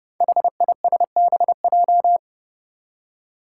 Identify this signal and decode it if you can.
Morse code, heard as a keyed tone, about 32 wpm, 700 Hz. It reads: HISBJ
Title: Five Challenging Letter-Number Combinations 30wpm